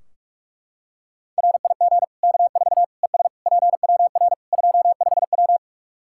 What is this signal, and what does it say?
Morse code, heard as a keyed tone, about 41 wpm, 700 Hz. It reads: RIG K4 ES PWR 25W
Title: Practice QSO - Exchange 2 - Info - 45wpm